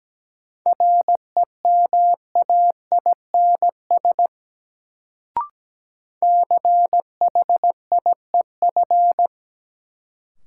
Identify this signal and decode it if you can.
Morse code, heard as a keyed tone, about 17 wpm, 700 Hz. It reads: REMAINS CHIEF
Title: Sets of 5 Words Encore - New Word Drill - 17wpm